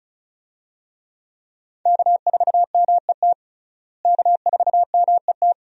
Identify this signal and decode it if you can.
Morse code, heard as a keyed tone, about 34 wpm, 700 Hz. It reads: K4MET K4MET
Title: USA Call Signs - 5 Characters - Repeated 2x - 35wpm